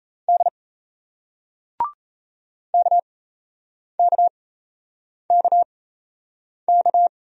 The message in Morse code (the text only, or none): D KKKK